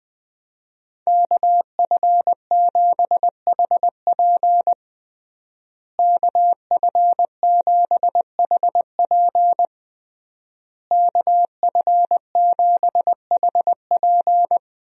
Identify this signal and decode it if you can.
Morse code, heard as a keyed tone, about 21 wpm, 700 Hz. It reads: KF7HP KF7HP KF7HP
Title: USA Call Signs - 5 Characters - Repeated 3x - 20wpm